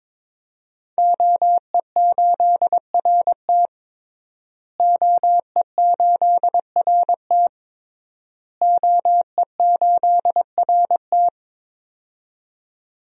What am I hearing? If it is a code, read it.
Morse code, heard as a keyed tone, about 21 wpm, 700 Hz. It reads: OE8RT OE8RT OE8RT
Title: DXCC-100 Call Signs - 5 Characters - Repeated 3x - 22wpm